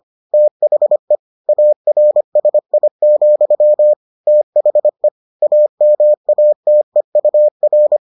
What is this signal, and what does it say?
Morse code, heard as a keyed tone, about 25 wpm, 600 Hz. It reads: THE ARSI, THE AMATEUR